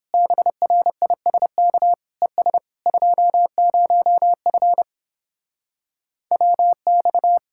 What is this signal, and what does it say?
Morse code, heard as a keyed tone, about 31 wpm, 700 Hz. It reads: BRISK ES 20F WX